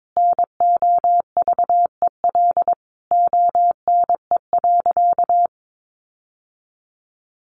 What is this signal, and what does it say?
Morse code, heard as a keyed tone, about 22 wpm, 700 Hz. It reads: NOVEL ONE.
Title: Sentences from Top 1300 Words 22wpm